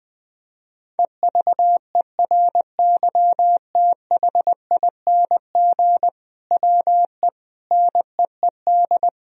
Morse code, heard as a keyed tone, about 21 wpm, 700 Hz. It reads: EVERYTHING WE NEED
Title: Sentences from Top 1000 Words - Review Part 1 - 20wpm